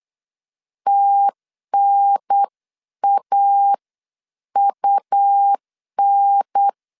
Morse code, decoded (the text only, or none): TNAUN